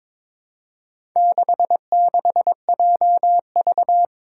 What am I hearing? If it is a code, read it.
Morse code, heard as a keyed tone, about 23 wpm, 700 Hz. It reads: 66JV
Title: Four Challenging Letter-Number Combinations 22wpm